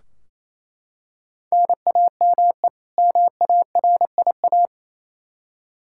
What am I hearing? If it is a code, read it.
Morse code, heard as a keyed tone, about 28 wpm, 700 Hz. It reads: NAME MARIA
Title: Practice QSO - Exchange 1 - Beginning - 28wpm